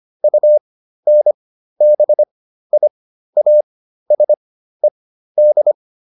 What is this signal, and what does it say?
Morse code, heard as a keyed tone, about 25 wpm, spaced as for 13 wpm, 600 Hz. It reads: UNBIASED